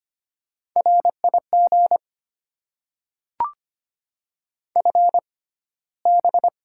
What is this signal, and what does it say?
Morse code, heard as a keyed tone, about 25 wpm, spaced as for 15 wpm, 700 Hz. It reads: RIG FB
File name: Top 20 QSO Elements - 6x Character Spacing - 25wpm